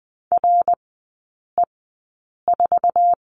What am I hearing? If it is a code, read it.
Morse code, heard as a keyed tone, about 20 wpm, spaced as for 8 wpm, 700 Hz. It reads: RE4
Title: 060 Lesson 11e - Letter h - Three Characters 20wpm